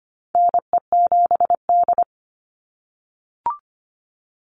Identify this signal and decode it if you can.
Morse code, heard as a keyed tone, about 25 wpm, 700 Hz. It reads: NE7D
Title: Call Signs - 25wpm